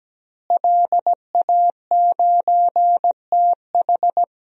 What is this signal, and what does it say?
Morse code, heard as a keyed tone, about 17 wpm, 700 Hz. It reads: LA9TH
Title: DXCC-100 Call Signs - 5 Characters - Repeated 3x - 17wpm